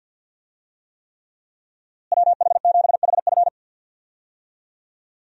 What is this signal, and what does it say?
Morse code, heard as a keyed tone, about 52 wpm, 700 Hz. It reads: WH7HF
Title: USA Call Signs - 5 Characters - Repeated 3x - 50wpm